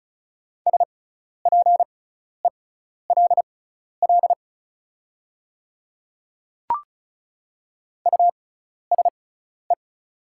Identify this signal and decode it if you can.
Morse code, heard as a keyed tone, about 36 wpm, spaced as for 12 wpm, 700 Hz. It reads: SPELL USE
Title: 6x Character Spacing - Top 300 Words - 35wpm